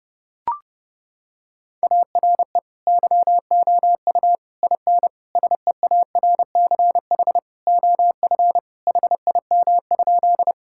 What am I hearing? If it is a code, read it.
Morse code, heard as a keyed tone, about 32 wpm, 700 Hz. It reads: ARE YOU IN SEARCH OF HIM?
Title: Sentences from Top 1000 Words - Review Part 5 - 30wpm